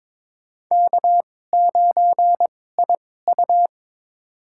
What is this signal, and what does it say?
Morse code, heard as a keyed tone, about 22 wpm, spaced as for 16 wpm, 700 Hz. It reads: K9IU
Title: USA Call Signs - 4 Characters - 2x Character Spacing - 22wpm